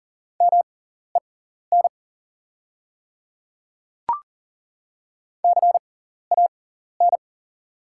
Morse code, heard as a keyed tone, about 40 wpm, spaced as for 14 wpm, 690 Hz. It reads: MEN CAN
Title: 6x Character Spacing - Top 200 Words - 40wpm